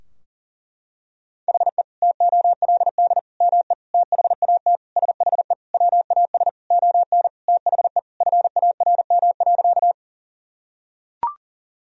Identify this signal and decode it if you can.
Morse code, heard as a keyed tone, about 40 wpm, 700 Hz. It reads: HE TOLD ME THAT SHE WAS ON THE FARM.
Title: Sentences from Top 1000 Words - Review Part 2 - 40wpm